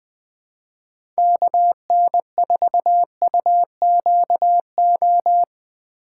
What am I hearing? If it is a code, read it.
Morse code, heard as a keyed tone, about 23 wpm, 700 Hz. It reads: KN4UQO
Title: Speed Racing - USA 2x3 Call Signs 20wpm